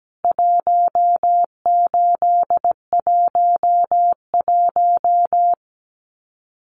Morse code, heard as a keyed tone, about 17 wpm, 700 Hz. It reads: 1811
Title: CWT Exchange 17wpm